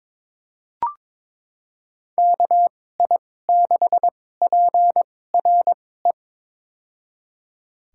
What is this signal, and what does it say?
Morse code, heard as a keyed tone, about 22 wpm, spaced as for 16 wpm, 700 Hz. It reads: KI6PRE
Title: USA Call Signs - 2x Character Spacing - 22wpm